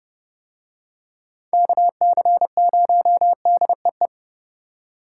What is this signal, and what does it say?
Morse code, heard as a keyed tone, about 29 wpm, 700 Hz. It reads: KC0DEE